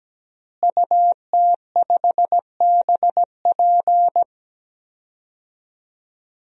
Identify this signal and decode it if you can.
Morse code, heard as a keyed tone, about 17 wpm, 700 Hz. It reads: UT5BP
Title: DXCC-100 Call Signs - 5 Characters - Repeated 3x - 17wpm